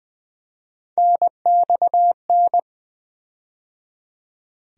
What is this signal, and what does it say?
Morse code, heard as a keyed tone, about 20 wpm, 700 Hz. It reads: NXN
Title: USA Call Sign Postfixes - 3 Letters 20wpm